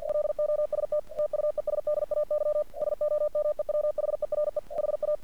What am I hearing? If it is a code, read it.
Morse code, heard as a keyed tone, about 47 wpm, 600 Hz. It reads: LOST TUESDAY SOMEWHERE HA
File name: archived / CRSnet